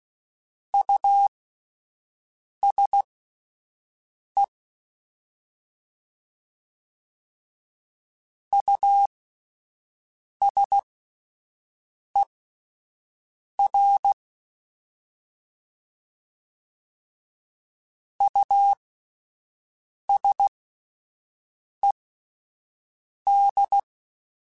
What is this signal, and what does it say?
Morse code, heard as a keyed tone, about 16 wpm, spaced as for 5 wpm, 800 Hz. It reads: USE USER USED
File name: Alphabetical / Alphabetical mp3s / Us @5